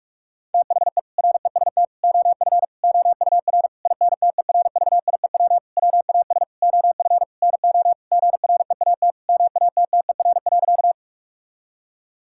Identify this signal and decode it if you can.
Morse code, heard as a keyed tone, about 41 wpm, 700 Hz. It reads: THE REST OF OUR INTERVIEW WAS OF NO GREAT MATTER.
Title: Sentences from Top 1600 Words 45wpm